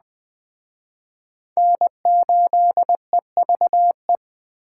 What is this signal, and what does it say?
Morse code, heard as a keyed tone, about 20 wpm, 700 Hz. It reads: N8EVE
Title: Speed Racing - USA 1x3 Call Signs 17wpm